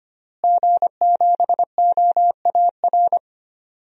Morse code, heard as a keyed tone, about 25 wpm, 700 Hz. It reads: G7OAR
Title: DXCC-100 Call Signs - 5 Characters - Repeated 3x - 25wpm